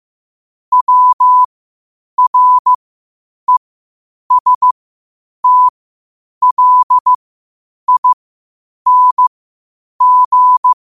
Morse code, heard as a keyed tone, about 15 wpm, spaced as for 8 wpm, 1000 Hz. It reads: WRESTLING
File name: qotdfiles / 8wpm